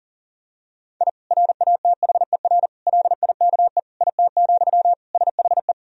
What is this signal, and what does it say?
Morse code, heard as a keyed tone, about 40 wpm, 700 Hz. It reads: I RATHER LIKE IT, SHE
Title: Sentences from Top 1000 Words - Review Part 4 - 40wpm